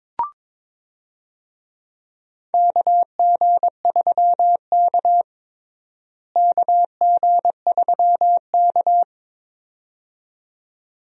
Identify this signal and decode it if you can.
Morse code, heard as a keyed tone, about 21 wpm, 700 Hz. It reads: KG3K KG3K
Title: USA Call Signs - 4 Characters - Repeated 2x - 22wpm